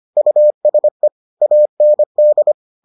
Morse code, heard as a keyed tone, about 25 wpm, 600 Hz. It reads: USE AND